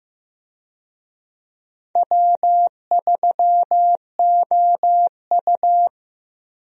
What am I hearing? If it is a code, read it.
Morse code, heard as a keyed tone, about 15 wpm, 700 Hz. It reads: W3OU
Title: USA Call Signs - 4 Characters - Repeated 2x - 15wpm